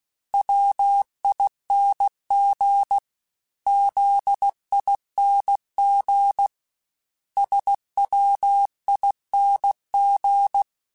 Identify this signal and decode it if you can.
Morse code, heard as a keyed tone, about 16 wpm, 800 Hz. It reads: WING ZING SWING
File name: Special INGs / Special INGs-mp3s / Sngl ing @15